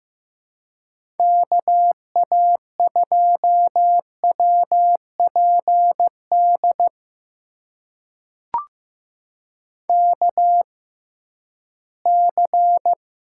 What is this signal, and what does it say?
Morse code, heard as a keyed tone, about 15 wpm, spaced as for 11 wpm, 700 Hz. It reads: KA2WPD KC
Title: USA Call Signs - 6 Characters - 6x Character Spacing - 15wpm